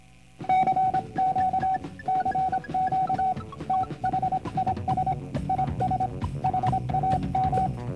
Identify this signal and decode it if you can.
Morse code, heard as a keyed tone, about 26 wpm, 760 Hz, with noise. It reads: CQ CQ THIS IS HAM